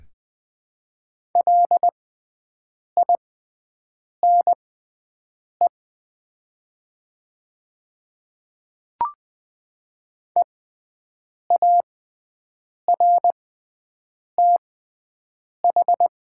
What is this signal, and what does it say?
Morse code, heard as a keyed tone, about 20 wpm, spaced as for 7 wpm, 700 Hz. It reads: LINE EARTH
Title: 6x Character Spacing - Top 200 Words - 20wpm